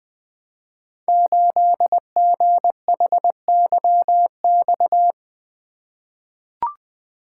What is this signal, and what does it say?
Morse code, heard as a keyed tone, about 20 wpm, 700 Hz. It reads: 8GHYX
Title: Code Groups - 5 Letter:Numbers - 20wpm